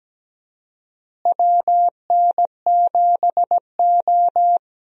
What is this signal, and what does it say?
Morse code, heard as a keyed tone, about 17 wpm, 700 Hz. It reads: WN7O
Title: USA Call Signs - 4 Characters - Repeated 3x - 17wpm